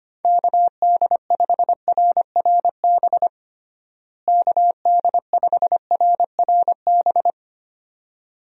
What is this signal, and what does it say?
Morse code, heard as a keyed tone, about 25 wpm, 700 Hz. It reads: KD5RRB KD5RRB
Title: USA Call Signs - 6 Characters - Repeated 2x - 25wpm